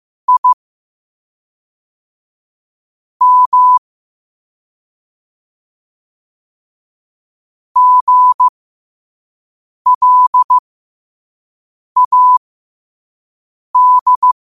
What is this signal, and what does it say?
Morse code, heard as a keyed tone, about 15 wpm, spaced as for 5 wpm, 1000 Hz. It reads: I M GLAD
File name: qotdfiles / 5wpm